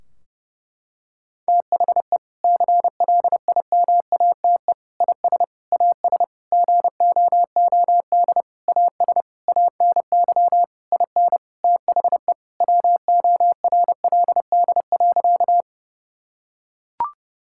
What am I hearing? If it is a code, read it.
Morse code, heard as a keyed tone, about 30 wpm, 700 Hz. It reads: THE CLIMATE IS AS GOOD AS ANY IN THE WORLD.
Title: Sentences from Top 1600 Words 30wpm